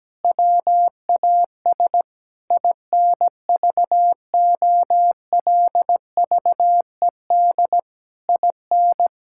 Morse code, heard as a keyed tone, about 17 wpm, 700 Hz. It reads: WAS INVOLVED IN
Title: Sentences from Top 1000 Words - Review Part 5 - 17wpm